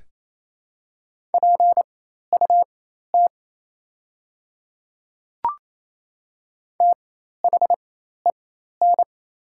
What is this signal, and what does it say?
Morse code, heard as a keyed tone, about 28 wpm, spaced as for 13 wpm, 700 Hz. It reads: PUT THEN
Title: 4x Character Spacing - Top 300 Words - 28wpm